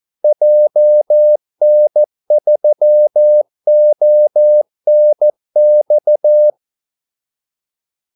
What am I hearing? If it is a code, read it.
Morse code, heard as a keyed tone, about 14 wpm, 600 Hz. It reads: JN3ONX
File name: Callsigns-14wpm